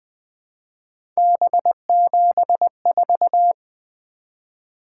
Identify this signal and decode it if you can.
Morse code, heard as a keyed tone, about 20 wpm, 700 Hz. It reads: B74
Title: Three Letter-Number Combinations - 20wpm